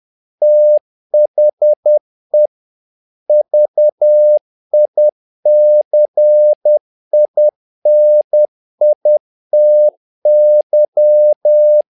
Morse code, heard as a keyed tone, about 10 wpm, 600 Hz. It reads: THE VICINITY
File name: learning center / tower trouble / tower trouble 10wpm